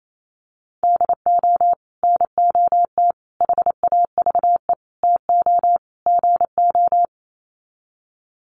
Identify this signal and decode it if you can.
Morse code, heard as a keyed tone, about 28 wpm, 700 Hz. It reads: DO NOT HAVE TO GO
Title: Sets of 5 Words 28wpm